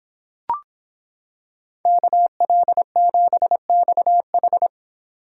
Morse code, heard as a keyed tone, about 26 wpm, 700 Hz. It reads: KL7XH